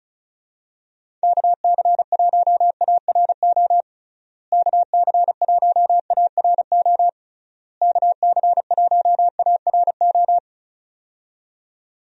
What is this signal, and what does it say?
Morse code, heard as a keyed tone, about 36 wpm, 700 Hz. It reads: KC1ARO KC1ARO KC1ARO
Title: USA Call Signs - 6 Characters - Repeated 3x - 35wpm